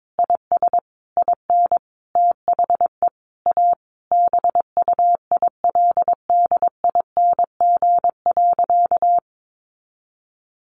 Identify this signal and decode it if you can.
Morse code, heard as a keyed tone, about 22 wpm, 700 Hz. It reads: IS IN THE A BUILDING.
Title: Additional Easy Phrases 22wpm